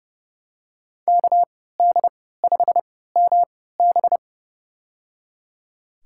Morse code, heard as a keyed tone, about 32 wpm, spaced as for 17 wpm, 700 Hz. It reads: KD5MB
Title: USA Call Signs - 5 Characters - 3x Character Spacing - 30wpm